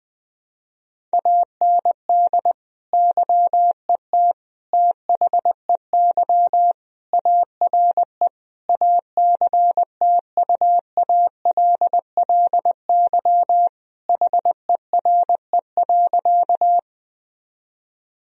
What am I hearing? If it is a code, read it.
Morse code, heard as a keyed tone, about 20 wpm, 700 Hz. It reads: AND YET THEY ARE ACTUALLY HERE.
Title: Sentences from Top 1000 Words - Review Part 4 - 20wpm